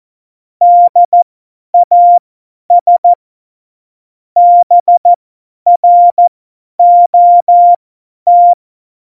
Morse code, heard as a keyed tone, about 14 wpm, spaced as for 10 wpm, 700 Hz. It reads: DAS BROT
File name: Morsekurs / Witze / Witz 18. Februar 2020, 50 F70